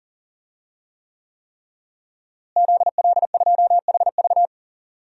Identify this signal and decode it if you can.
Morse code, heard as a keyed tone, about 38 wpm, 700 Hz. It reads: ZL2HV
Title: DXCC-100 Call Signs - 5 Characters - Repeated 3x - 40wpm